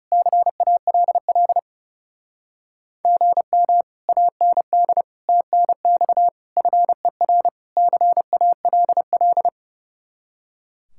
Morse code, heard as a keyed tone, about 37 wpm, 700 Hz. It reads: CALL GM AND TNX FER CALL
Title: Speed-Racing Common QSO Phrases - 30wpm